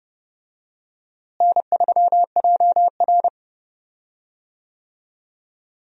Morse code, heard as a keyed tone, about 30 wpm, 700 Hz. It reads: N3JR
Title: USA Call Signs - 4 Characters - Repeated 3x - 30wpm